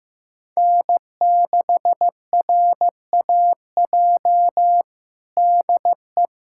Morse code, heard as a keyed tone, about 15 wpm, 700 Hz. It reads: N6RAJ DE
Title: USA Callsign DE Callsign - 5 Characters - 15wpm